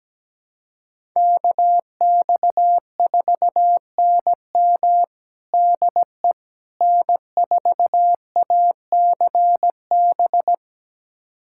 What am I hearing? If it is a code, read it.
Morse code, heard as a keyed tone, about 17 wpm, 700 Hz. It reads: KX4NM DE N4ACB
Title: USA Callsign DE Callsign 17wpm